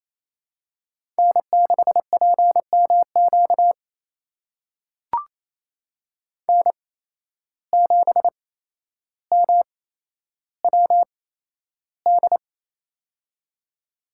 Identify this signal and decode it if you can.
Morse code, heard as a keyed tone, about 28 wpm, spaced as for 12 wpm, 700 Hz. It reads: N6PMQ N7MWD
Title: USA Call Signs - 5 Characters - 8x Character Spacing - 28wpm